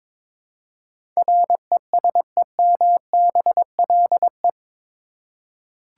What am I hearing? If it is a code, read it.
Morse code, heard as a keyed tone, about 22 wpm, 700 Hz. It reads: RESEMBLE
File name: practice / 22wpm / Eight Letter Words - 22wpm